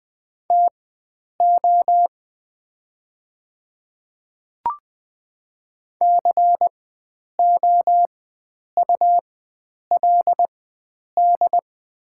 Morse code, heard as a keyed tone, about 20 wpm, spaced as for 9 wpm, 700 Hz. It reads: TO COULD
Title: 4x Character Spacing - Top 100 Words - 20wpm